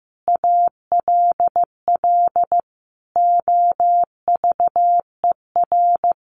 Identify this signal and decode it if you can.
Morse code, heard as a keyed tone, about 15 wpm, 700 Hz. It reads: ALL OVER
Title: Sentences from Top 1300 Words 15wpm